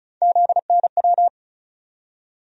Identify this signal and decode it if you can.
Morse code, heard as a keyed tone, about 34 wpm, 700 Hz. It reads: ZNW
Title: USA Call Sign Postfixes - 3 Letters 35wpm